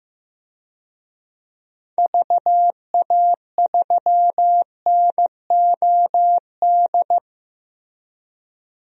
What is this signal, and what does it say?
Morse code, heard as a keyed tone, about 15 wpm, 700 Hz. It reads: VA3NOD
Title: DXCC-100 Call Signs - 6 Characters - Repeated 2x - 15wpm